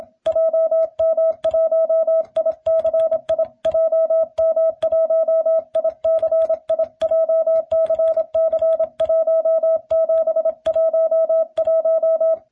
Morse code, heard as a keyed tone, about 27 wpm, 650 Hz. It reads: JM1ICI JM1ICI JCC1711